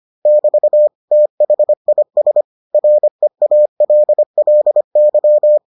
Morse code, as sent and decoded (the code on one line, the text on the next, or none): -...- / - .... .. ... / .-. .- .-.. .-.. -.--
<BT> THIS REALLY